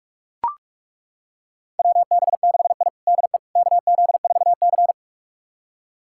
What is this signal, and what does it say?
Morse code, heard as a keyed tone, about 42 wpm, 700 Hz. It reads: WB6I DE KZ4C
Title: USA Callsign DE Callsign - 4 Characters - 45wpm